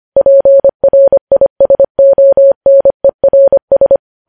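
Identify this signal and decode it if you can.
Morse code, heard as a keyed tone, about 25 wpm, 550 Hz. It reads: PRISONERS
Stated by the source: Morse code news headlines sent at 25 wpm.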